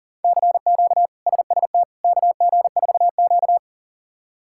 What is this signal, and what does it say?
Morse code, heard as a keyed tone, about 40 wpm, 700 Hz. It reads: CQ SST KG4Q
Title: K1USN SST QSO Practice - 40wpm